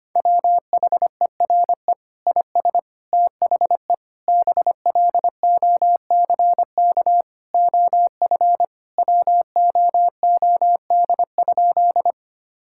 Morse code, heard as a keyed tone, about 25 wpm, 700 Hz. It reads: WHERE IS THE BLOCK OF WOOD?
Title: Sentences from Top 1600 Words 25wpm